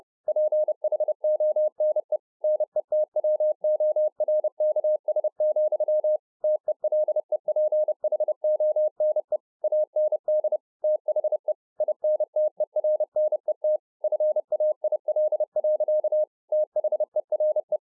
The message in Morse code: PHONE NETWORKS, TELEPHONE AND THE INTERNET FAIL. THERE

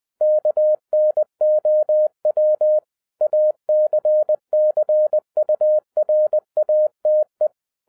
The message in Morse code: KNOW ACCURATE

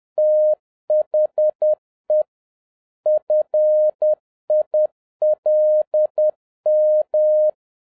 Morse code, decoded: THE FILM

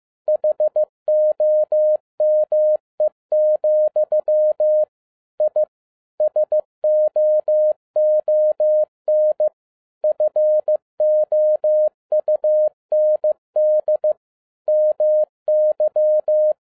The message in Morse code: HOME, I SOON FOUND MY

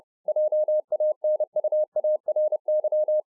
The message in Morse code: JANUARY

